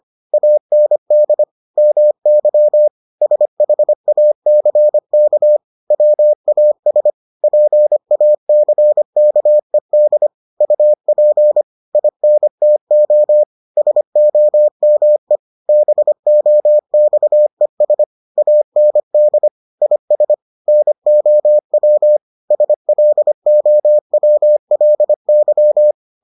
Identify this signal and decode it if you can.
Morse code, heard as a keyed tone, about 25 wpm, 600 Hz. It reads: AND MY SHACK WAS PACKED UP INTO SOME BOXES AND IS NOW SLOWLY